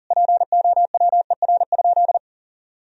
Morse code, heard as a keyed tone, about 37 wpm, 700 Hz. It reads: POWER?
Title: Sentences from Top 1000 Words - Review Part 2 - 40wpm